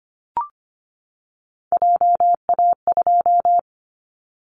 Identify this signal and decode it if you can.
Morse code, heard as a keyed tone, about 25 wpm, 700 Hz. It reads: JA2
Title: Callsign Prefixes for Top 25 DXCC - 25wpm